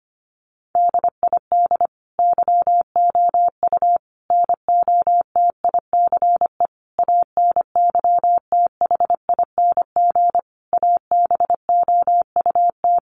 Morse code, heard as a keyed tone, about 25 wpm, 700 Hz. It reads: DID YOU NOTICE ANYTHING ABOUT
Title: Sentences from Top 1000 Words 25wpm